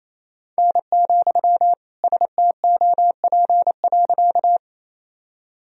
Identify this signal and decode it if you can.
Morse code, heard as a keyed tone, about 28 wpm, 700 Hz. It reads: N, STOP.
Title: Sentences from Top 300 Words and Abbreviations 28wpm